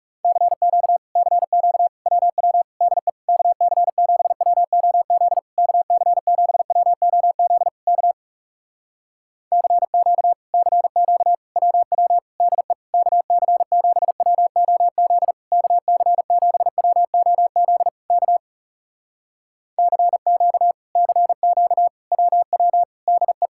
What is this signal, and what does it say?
Morse code, heard as a keyed tone, about 40 wpm, 700 Hz. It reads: CQ CQ WW DE KC7WOZ KC7WOZ K CQ CQ WW DE KC7WOZ KC7WOZ K CQ CQ WW DE